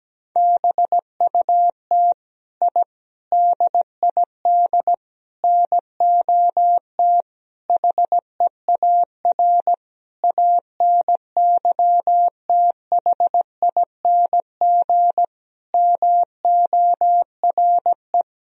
Morse code, heard as a keyed tone, about 17 wpm, 700 Hz. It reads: BUT I DID NOT HEAR ANYTHING MORE
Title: Sentences from Top 1000 Words - Review Part 4 - 17wpm